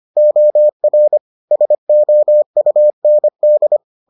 Morse code, heard as a keyed tone, about 25 wpm, 600 Hz. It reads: OR SOUND